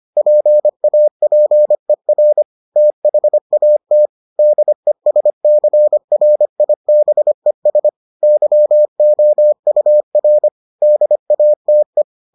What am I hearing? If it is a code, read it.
Morse code, heard as a keyed tone, about 25 wpm, 600 Hz. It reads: PAPER THAT DESCRIBES YOUR DATE